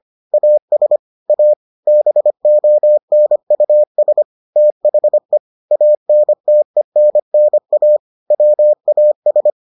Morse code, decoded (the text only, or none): AS A BONUS THE ANTENNA WAS